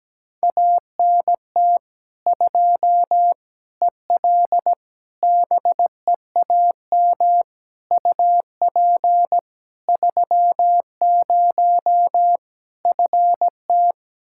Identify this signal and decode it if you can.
Morse code, heard as a keyed tone, about 17 wpm, 700 Hz. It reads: ANT 2 EL BEAM UP 30 FT
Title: Practice QSO - Exchange 2 - Info - 17wpm